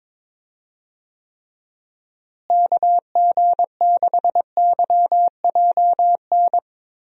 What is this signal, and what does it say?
Morse code, heard as a keyed tone, about 22 wpm, 700 Hz. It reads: KG6YJN
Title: USA Call Signs - 6 Characters - Repeated 2x - 22wpm